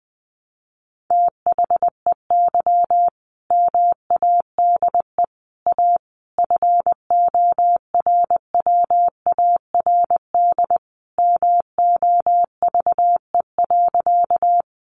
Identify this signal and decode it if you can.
Morse code, heard as a keyed tone, about 19 wpm, 700 Hz. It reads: THEY MADE A FORWARD MOVE.
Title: Sentences from Top 1300 Words 20wpm